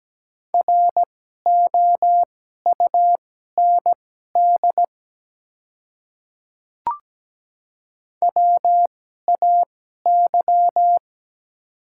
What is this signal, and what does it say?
Morse code, heard as a keyed tone, about 17 wpm, spaced as for 12 wpm, 700 Hz. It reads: ROUND WAY
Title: 2x Character Spacing - Top 200 Words - 17wpm